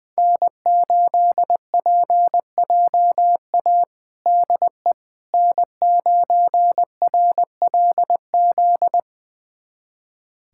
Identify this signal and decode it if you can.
Morse code, heard as a keyed tone, about 20 wpm, 700 Hz. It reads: N8PJA DE N9RLZ
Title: USA Callsign DE Callsign - 5 Characters - 20wpm